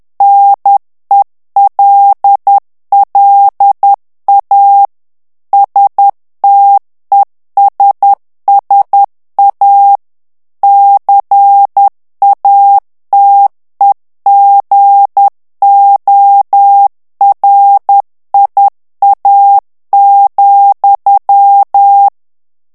Morse code, heard as a keyed tone, about 10 wpm, 800 Hz. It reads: NELLA STESSA CATEGORIA,
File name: ITALIEN texte clair 2 vitesse 10 mots minute